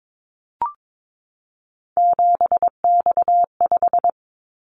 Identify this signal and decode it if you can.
Morse code, heard as a keyed tone, about 22 wpm, 700 Hz. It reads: 7X5
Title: Callsign Prefixes for Top 100 DXCC - 22wpm